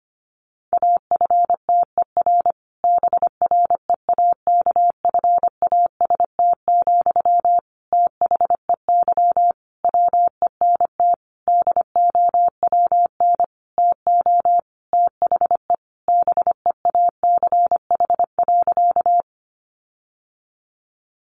Morse code, heard as a keyed tone, about 25 wpm, 700 Hz. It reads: AFTER BREAKFAST, THEY WENT DOWN TO THE BEACH.
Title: Sentences from Top 1500 Words 25wpm